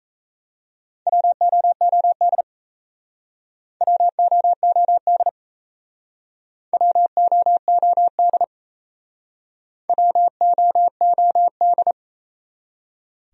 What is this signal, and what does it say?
Morse code, heard as a keyed tone, about 37 wpm, 700 Hz. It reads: WOOD WOOD WOOD WOOD